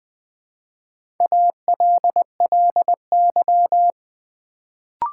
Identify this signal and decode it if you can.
Morse code, heard as a keyed tone, about 20 wpm, 700 Hz. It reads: ALLY
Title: Build A Word - 1 Letter At A Time - 3 to 5-Letter Words - 20wpm